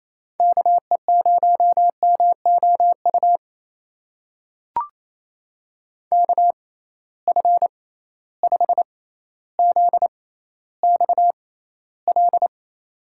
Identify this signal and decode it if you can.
Morse code, heard as a keyed tone, about 28 wpm, spaced as for 14 wpm, 700 Hz. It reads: KE0MOU KF5ZXL
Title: USA Call Signs - 6 Characters - 6x Character Spacing - 28wpm